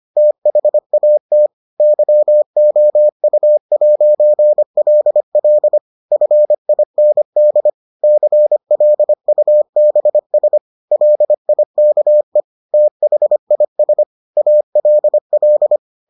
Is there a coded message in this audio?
Yes: Morse code from 0.2 to 15.8 s, heard as a keyed tone, about 25 wpm, 600 Hz. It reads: THAT YOU'LL FIND CLUBS LIKE THIS ALL